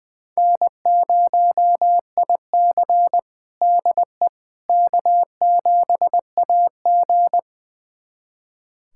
Morse code, heard as a keyed tone, about 20 wpm, 700 Hz. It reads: N0IC DE K7AG
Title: USA Callsign DE Callsign - 4 Characters - 20wpm